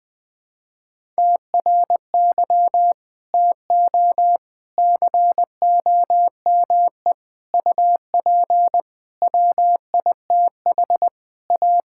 Morse code, heard as a keyed tone, about 21 wpm, 700 Hz. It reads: TRY TO COME UP WITH A
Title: Sentences with 4-Word Phrases - 20wpm